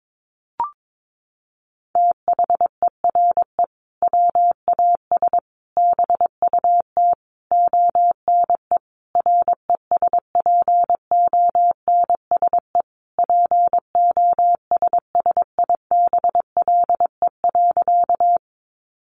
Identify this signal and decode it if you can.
Morse code, heard as a keyed tone, about 22 wpm, 700 Hz. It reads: THERE WAS BUT ONE RESPONSE POSSIBLE.
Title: Sentences from Top 1400 Words 22wpm